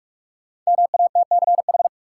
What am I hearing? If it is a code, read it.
Morse code, heard as a keyed tone, about 42 wpm, 700 Hz. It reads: MATCH
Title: Sentences from Top 1600 Words - New Word Drill - 45wpm